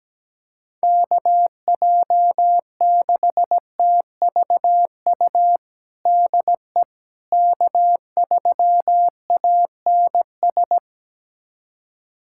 Morse code, heard as a keyed tone, about 17 wpm, 700 Hz. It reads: KJ6TVU DE K3ANS
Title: USA Callsign DE Callsign 17wpm